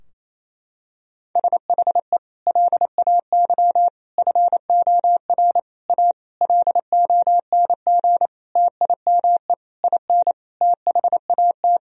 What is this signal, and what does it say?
Morse code, heard as a keyed tone, about 28 wpm, 700 Hz. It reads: SHE LAY FOR A LONG TIME IN THAT
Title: Sentences from Top 1000 Words - Review Part 2 - 28wpm